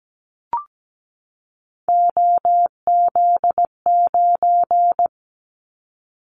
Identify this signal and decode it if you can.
Morse code, heard as a keyed tone, about 17 wpm, 700 Hz. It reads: OZ9
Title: Callsign Prefixes for Top 50 DXCC - 17wpm